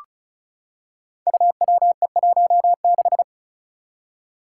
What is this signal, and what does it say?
Morse code, heard as a keyed tone, about 36 wpm, 700 Hz. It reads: UWE16
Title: Five Challenging Letter-Number Combinations 35wpm